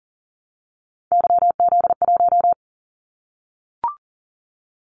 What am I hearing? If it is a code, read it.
Morse code, heard as a keyed tone, about 38 wpm, 700 Hz. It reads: YZ1
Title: Callsign Prefixes for Top 25 DXCC - 40wpm